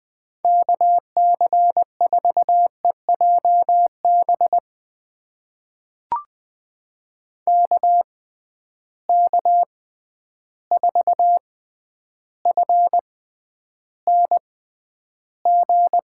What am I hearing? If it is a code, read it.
Morse code, heard as a keyed tone, about 20 wpm, spaced as for 10 wpm, 700 Hz. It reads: KC4EJB KK4FNG
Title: USA Call Signs - 6 Characters - 6x Character Spacing - 20wpm